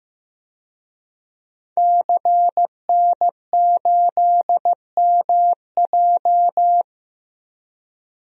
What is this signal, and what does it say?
Morse code, heard as a keyed tone, about 15 wpm, 700 Hz. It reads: CN8MJ
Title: DXCC-100 Call Signs - 5 Characters - Repeated 2x - 15wpm